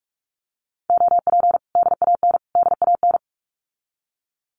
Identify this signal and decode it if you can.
Morse code, heard as a keyed tone, about 44 wpm, 700 Hz. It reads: OP DAN DAN
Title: Common QSO Phrases Encore 45wpm